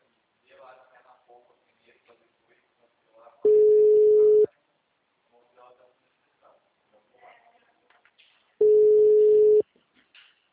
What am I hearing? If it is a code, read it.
Ringback tone (Europe, 425 Hz), with speech.
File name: out- -2031- - - .4295